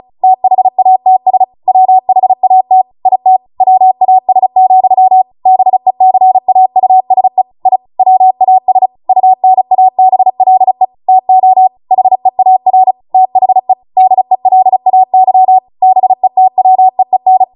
Morse code, heard as a keyed tone, about 36 wpm, 750 Hz. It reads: THATS WHAT IT WAS, BECAUSE I WAS UNABLE TO HEAR THE DELAY BETWEEN